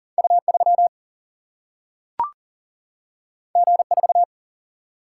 Morse code, heard as a keyed tone, about 38 wpm, 700 Hz. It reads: U 3 G 4
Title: Single-Letter Single-Number 40wpm